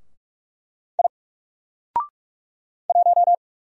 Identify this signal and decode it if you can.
Morse code, heard as a keyed tone, about 44 wpm, 700 Hz. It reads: I 1
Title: Single Character (Letter:Number:Punctuation:Prosign) 45wpm